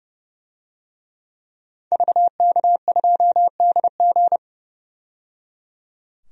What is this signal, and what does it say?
Morse code, heard as a keyed tone, about 31 wpm, 700 Hz. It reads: VK2DG
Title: DXCC-100 Call Signs - 5 Characters - 30wpm